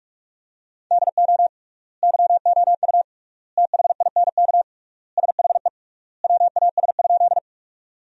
Morse code, heard as a keyed tone, about 42 wpm, 700 Hz. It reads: DO YOU THINK SHE WAS?
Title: 3x Word-Spacing - Sentences from Top 200 Words - 45wpm